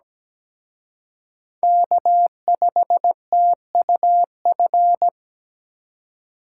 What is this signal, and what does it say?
Morse code, heard as a keyed tone, about 20 wpm, 700 Hz. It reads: K5TUF
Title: Speed Racing - USA 1x3 Call Signs 17wpm